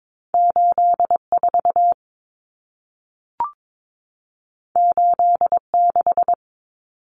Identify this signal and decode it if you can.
Morse code, heard as a keyed tone, about 22 wpm, 700 Hz. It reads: 84 86
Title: Numbers - 2-Digit 22wpm